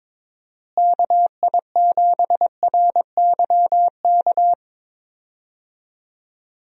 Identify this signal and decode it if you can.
Morse code, heard as a keyed tone, about 22 wpm, 700 Hz. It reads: KI7RYK
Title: USA Call Signs - 6 Characters - Repeated 2x - 22wpm